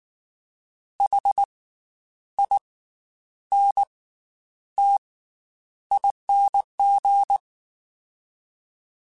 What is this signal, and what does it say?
Morse code, heard as a keyed tone, about 19 wpm, spaced as for 9 wpm, 800 Hz. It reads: HINTING
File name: Special INGs / Special INGs-mp3s / Slow ing @18